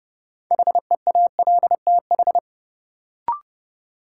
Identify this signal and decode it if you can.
Morse code, heard as a keyed tone, about 30 wpm, 700 Hz. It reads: HEALTH
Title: Sets of 5 Words - New Word Drill - 30wpm